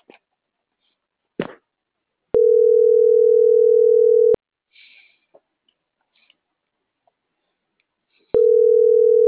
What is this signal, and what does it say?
Ringback tone (North America), 440 + 480 Hz.